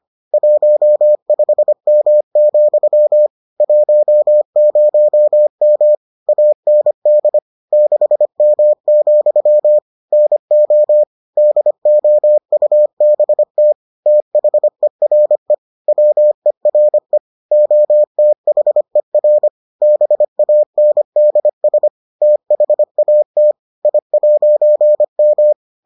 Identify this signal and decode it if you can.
Morse code, heard as a keyed tone, about 25 wpm, 600 Hz. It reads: 15M, 10M AND 6M, NO DOUBT THERE WERE OTHER BANDS THAT I'M